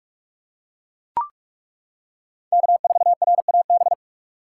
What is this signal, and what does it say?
Morse code, heard as a keyed tone, about 41 wpm, 700 Hz. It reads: K4RAB
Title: K1USN SST QSO Practice - 45wpm